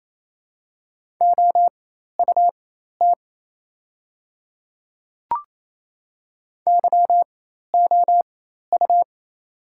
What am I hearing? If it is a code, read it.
Morse code, heard as a keyed tone, about 28 wpm, spaced as for 13 wpm, 700 Hz. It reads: OUT YOU
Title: 4x Character Spacing - Top 100 Words - 28wpm